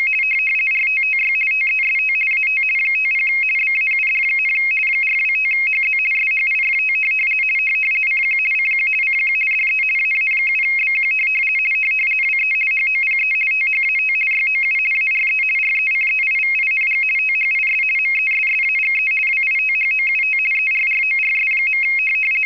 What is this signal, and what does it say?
RTTY, about 50 baud. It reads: RTTY TEST AT 50 BAUD, 425HZ SHIFT CQ CQ CQ DE AA7BQ WE HOPE THAT YOU ARE RECEIVING THIS MESSAGE WITHOUT ERRORS.